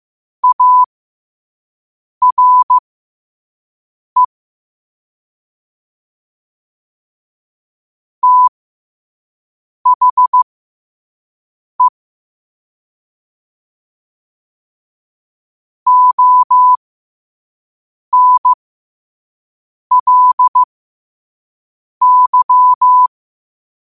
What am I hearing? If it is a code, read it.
Morse code, heard as a keyed tone, about 15 wpm, spaced as for 5 wpm, 1000 Hz. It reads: ARE THE ONLY
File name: qotdfiles / 5wpm